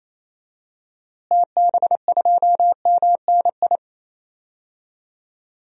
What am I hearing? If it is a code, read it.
Morse code, heard as a keyed tone, about 28 wpm, 700 Hz. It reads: TB2MNI
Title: DXCC-100 Call Signs - 6 Characters - Repeated 2x - 28wpm